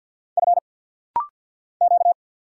Morse code, heard as a keyed tone, about 49 wpm, 700 Hz. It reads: F Q
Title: Single Letter-Number Alt - Rapid-Fire 50wpm